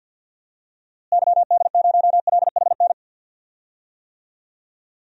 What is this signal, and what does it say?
Morse code, heard as a keyed tone, about 48 wpm, 700 Hz. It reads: YD0LHN
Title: DXCC-100 Call Signs - 6 Characters - Repeated 2x - 50wpm